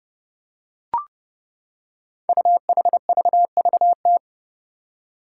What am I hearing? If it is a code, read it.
Morse code, heard as a keyed tone, about 30 wpm, 700 Hz. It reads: UHVVT
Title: Five Challenging Letter-Number Combinations 30wpm